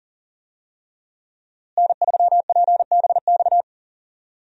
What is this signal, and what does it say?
Morse code, heard as a keyed tone, about 40 wpm, 700 Hz. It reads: N3PBX
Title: USA Call Signs - 5 Characters - Repeated 2x - 40wpm